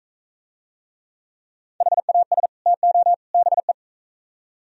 Morse code, heard as a keyed tone, about 42 wpm, 700 Hz. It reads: HAS TO BE